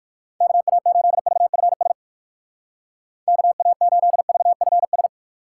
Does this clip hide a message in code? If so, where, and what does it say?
Yes: Morse code from 0.4 to 5.1 s, heard as a keyed tone, about 59 wpm, 700 Hz. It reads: KA8VFS KA8VFS